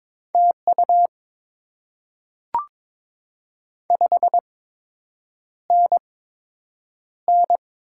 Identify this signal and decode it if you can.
Morse code, heard as a keyed tone, about 22 wpm, spaced as for 8 wpm, 700 Hz. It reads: TU 5NN
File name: Top 20 QSO Elements - 8x Character Spacing - 22wpm